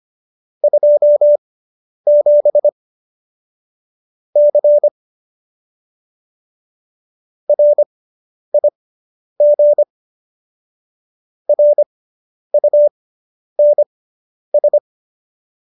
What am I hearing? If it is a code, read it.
Morse code, heard as a keyed tone, about 25 wpm, spaced as for 10 wpm, 590 Hz. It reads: 27 C RIG RUNS